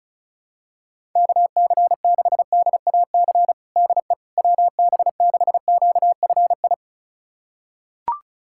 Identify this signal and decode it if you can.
Morse code, heard as a keyed tone, about 34 wpm, 700 Hz. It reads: KC6DAC DE WB6QFI
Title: USA Callsign DE Callsign - 6 Characters - 35wpm